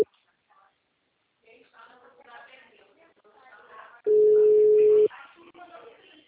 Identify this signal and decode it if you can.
Ringback tone (Europe, 425 Hz), with speech.